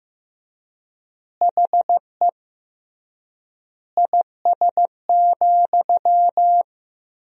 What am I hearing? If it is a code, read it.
Morse code, heard as a keyed tone, about 16 wpm, 700 Hz. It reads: HE IS,
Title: 3x Word-Spacing - Sentences from Top 100 Words - 15wpm